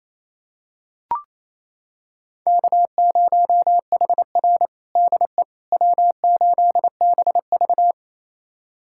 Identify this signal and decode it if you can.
Morse code, heard as a keyed tone, about 28 wpm, 700 Hz. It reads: K0HR DE W8BV
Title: USA Callsign DE Callsign - 4 Characters - 28wpm